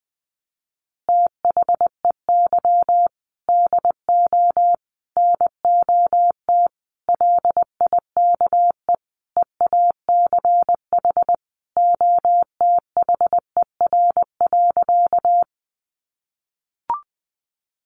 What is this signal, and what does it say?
Morse code, heard as a keyed tone, about 20 wpm, 700 Hz. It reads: THEY DO NOT LIKE EACH OTHER.
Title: Easy Phrases 20wpm